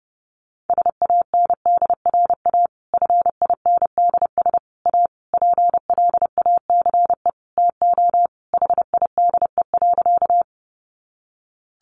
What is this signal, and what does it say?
Morse code, heard as a keyed tone, about 30 wpm, 700 Hz. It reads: SANDRA FINDS A PLACE TO HIDE.
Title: Easy Phrases 30wpm